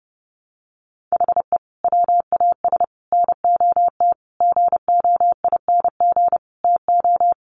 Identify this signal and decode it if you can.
Morse code, heard as a keyed tone, about 30 wpm, 700 Hz. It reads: HE WAS NOT GOING TO
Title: Sets of 5 Words 30wpm